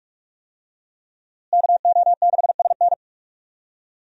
Morse code, heard as a keyed tone, about 46 wpm, 700 Hz. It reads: KO6SN
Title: USA Call Signs - 5 Characters - Repeated 3x - 45wpm